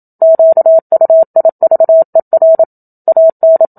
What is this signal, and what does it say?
Morse code, heard as a keyed tone, about 27 wpm, 650 Hz. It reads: QUIVER AN
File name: Book Two / 01-Chapter 1 - 27 WPM